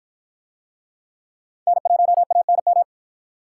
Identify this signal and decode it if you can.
Morse code, heard as a keyed tone, about 45 wpm, 700 Hz. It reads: N1ANM